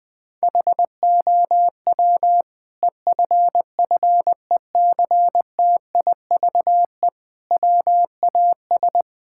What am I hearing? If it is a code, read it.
Morse code, heard as a keyed tone, about 20 wpm, 700 Hz. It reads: HOW EFFECTIVE WAS
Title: Sentences from Top 1000 Words - Review Part 6 - 20wpm